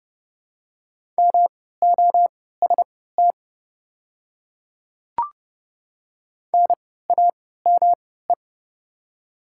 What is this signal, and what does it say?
Morse code, heard as a keyed tone, about 32 wpm, spaced as for 17 wpm, 700 Hz. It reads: MOST NAME
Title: 3x Character Spacing - Top 200 Words - 30wpm